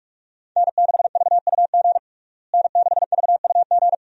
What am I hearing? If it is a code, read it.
Morse code, heard as a keyed tone, about 46 wpm, 700 Hz. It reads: N6VUG N6VUG
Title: USA Call Signs - 5 Characters - Repeated 2x - 45wpm